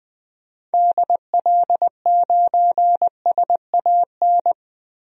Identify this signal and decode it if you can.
Morse code, heard as a keyed tone, about 20 wpm, 700 Hz. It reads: DL9SAN
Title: DXCC-100 Call Signs - 6 Characters - Repeated 2x - 20wpm